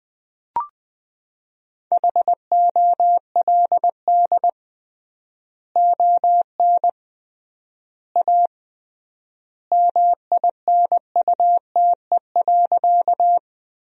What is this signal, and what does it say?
Morse code, heard as a keyed tone, about 21 wpm, 700 Hz. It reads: HOLD ON A MINUTE.
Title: 3x Word-Spacing - Sentences from Top 500 Words - 20wpm